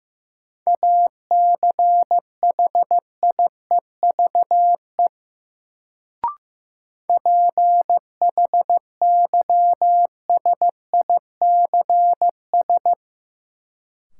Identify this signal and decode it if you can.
Morse code, heard as a keyed tone, about 15 wpm, 700 Hz. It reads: ACHIEVE PHYSICS
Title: Sentences from Top 1000 Words - Review Part 5 - New Word Drill - 15wpm